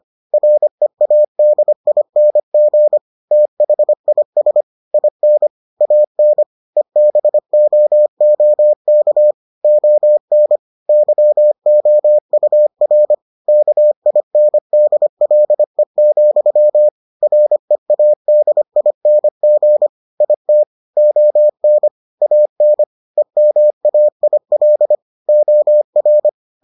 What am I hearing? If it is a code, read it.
Morse code, heard as a keyed tone, about 25 wpm, 600 Hz. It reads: READING THIS IN AN EBOOK ON YOUR KINDLE, READING IT ON AN EMAIL OR